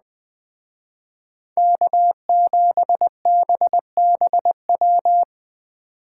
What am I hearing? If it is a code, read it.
Morse code, heard as a keyed tone, about 20 wpm, 700 Hz. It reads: K7BBW
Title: Speed Racing - USA 1x3 Call Signs 17wpm